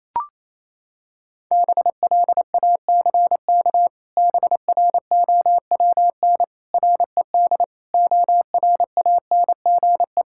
Morse code, heard as a keyed tone, about 28 wpm, 700 Hz. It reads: BLACK BROWN RED ORANGE
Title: Pangram Challenge v2 - 28wpm